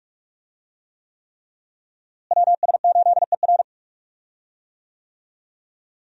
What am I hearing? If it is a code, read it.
Morse code, heard as a keyed tone, about 41 wpm, 700 Hz. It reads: WS8ER
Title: USA Call Signs - 5 Characters - Repeated 3x - 45wpm